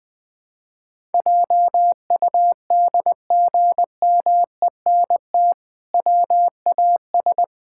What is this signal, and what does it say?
Morse code, heard as a keyed tone, about 21 wpm, 700 Hz. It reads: JUDGMENT WAS
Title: Sentences with 5-Word Phrases - 20wpm